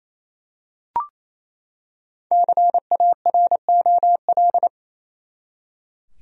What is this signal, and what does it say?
Morse code, heard as a keyed tone, about 28 wpm, 700 Hz. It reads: CAROL
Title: USA Common Names - Max 5 Letters - 28wpm